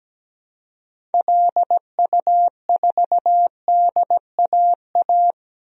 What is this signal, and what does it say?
Morse code, heard as a keyed tone, about 17 wpm, 700 Hz. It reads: LU4DAA
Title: DXCC-100 Call Signs - 6 Characters - Repeated 3x - 17wpm